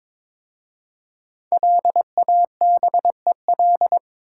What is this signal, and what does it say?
Morse code, heard as a keyed tone, about 22 wpm, 700 Hz. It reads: LABEL
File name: 5 Letter Words - First 3 Letters Given - 22wpm